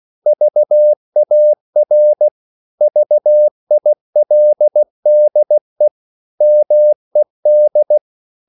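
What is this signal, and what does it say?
Morse code, heard as a keyed tone, about 16 wpm, 600 Hz. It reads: VAR VILDE MED
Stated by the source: Slow (QRS) Morse code at 16 wpm, from CW med Gnister.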